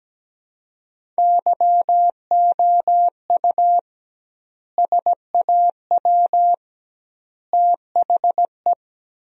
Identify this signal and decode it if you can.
Morse code, heard as a keyed tone, about 17 wpm, 700 Hz. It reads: YOU SAW THE
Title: 2x Word-Spacing - Sentences from Top 400 Words - 17wpm